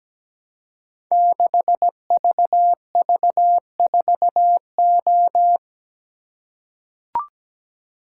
Morse code, heard as a keyed tone, about 17 wpm, 700 Hz. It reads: 6VV4O
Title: Code Groups - 5 Letter:Numbers - 17wpm